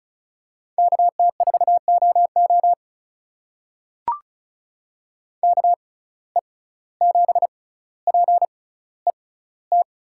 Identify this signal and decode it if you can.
Morse code, heard as a keyed tone, about 34 wpm, spaced as for 17 wpm, 700 Hz. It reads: KT4OO KE7PET
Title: USA Call Signs - 6x Character Spacing - 35wpm